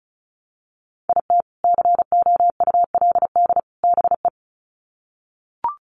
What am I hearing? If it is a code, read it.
Morse code, heard as a keyed tone, about 34 wpm, 700 Hz. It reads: IT COULD BE
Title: Sets of 3 Words 35wpm